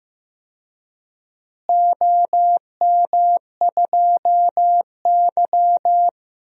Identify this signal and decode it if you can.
Morse code, heard as a keyed tone, about 15 wpm, 700 Hz. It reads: OM2Y
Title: DXCC-100 Call Signs - 4 Characters - Repeated 2x - 15wpm